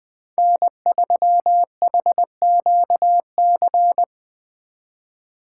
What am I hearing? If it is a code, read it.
Morse code, heard as a keyed tone, about 24 wpm, 700 Hz. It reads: N3HQC
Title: Speed Racing - USA 1x3 Call Signs 20wpm